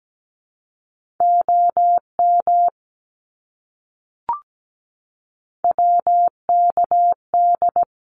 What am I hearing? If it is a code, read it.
Morse code, heard as a keyed tone, about 17 wpm, 700 Hz. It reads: OM WKD
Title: Abbreviations Repeated 3x - 17wpm